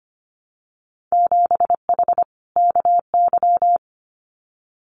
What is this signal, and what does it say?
Morse code, heard as a keyed tone, about 25 wpm, 700 Hz. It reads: 7H KY
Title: Winter Field Day Exchanges Encore 25wpm